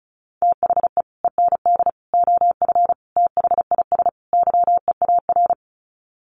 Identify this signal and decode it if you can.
Morse code, heard as a keyed tone, about 34 wpm, 700 Hz. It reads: THE END OF THIS YEAR
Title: Sets of 5 Words Encore 35wpm